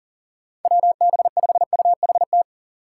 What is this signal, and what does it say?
Morse code, heard as a keyed tone, about 39 wpm, 700 Hz. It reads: WB5UHT
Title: USA Call Signs - 6 Characters - 3x Character Spacing - 40wpm